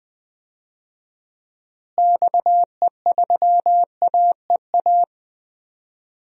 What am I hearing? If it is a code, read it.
Morse code, heard as a keyed tone, about 20 wpm, 700 Hz. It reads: XE3AEA
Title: DXCC-100 Call Signs - 6 Characters - Repeated 3x - 20wpm